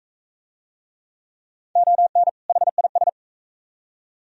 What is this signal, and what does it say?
Morse code, heard as a keyed tone, about 39 wpm, 700 Hz. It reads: ON HIS